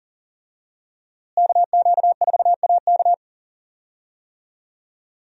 Morse code, heard as a keyed tone, about 40 wpm, 700 Hz. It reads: KQ4AK
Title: USA Call Signs - 5 Characters - Repeated 2x - 40wpm